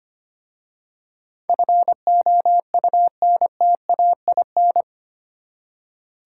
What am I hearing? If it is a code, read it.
Morse code, heard as a keyed tone, about 25 wpm, 700 Hz. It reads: FOUNTAIN
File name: practice / 25wpm / Easy Words with Hints - 25wpm